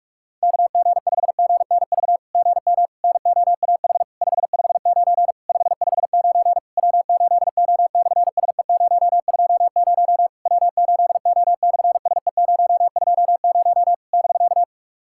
Morse code, heard as a keyed tone, about 44 wpm, 700 Hz. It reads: KG5GNV GM NOAH 559 559 W8O/SE020 W8O/SE020 <BK>
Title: SOTA QSO Practice 45wpm